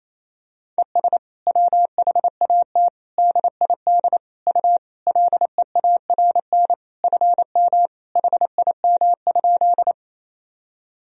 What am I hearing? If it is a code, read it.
Morse code, heard as a keyed tone, about 28 wpm, 700 Hz. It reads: ES WHAT DID U LEARN FM HIM?
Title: Sentences from Top 300 Words and Abbreviations 28wpm